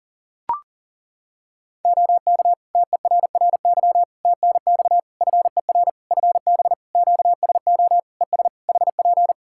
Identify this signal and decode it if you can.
Morse code, heard as a keyed tone, about 37 wpm, 700 Hz. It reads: OK TERRY TNX FER FB QSO ES HP
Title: Practice QSO - Exchange 4 - Ending - 40wpm